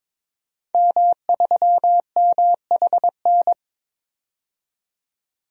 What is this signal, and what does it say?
Morse code, heard as a keyed tone, about 22 wpm, 700 Hz. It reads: M3MHN
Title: DXCC-100 Call Signs - 5 Characters - 22wpm